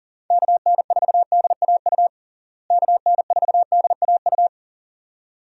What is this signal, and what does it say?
Morse code, heard as a keyed tone, about 40 wpm, 700 Hz. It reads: KN4DAU KN4DAU
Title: USA Call Signs - 6 Characters - Repeated 2x - 40wpm